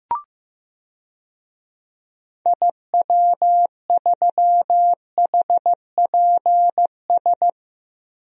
Morse code, heard as a keyed tone, about 15 wpm, 700 Hz. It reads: IW3HPS
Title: DXCC-100 Call Signs - 6 Characters - Repeated 2x - 15wpm